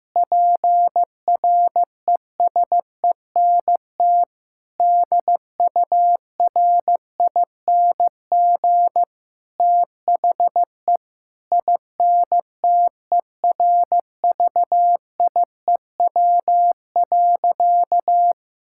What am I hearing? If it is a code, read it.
Morse code, heard as a keyed tone, about 15 wpm, 700 Hz. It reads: PRESENT DURING THE INTERVIEW.
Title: Sentences from Top 1600 Words 15wpm